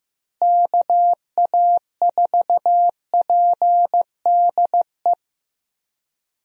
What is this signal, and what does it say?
Morse code, heard as a keyed tone, about 16 wpm, 700 Hz. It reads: KA4PDE
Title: USA Call Signs - 6 Characters - Repeated 2x - 15wpm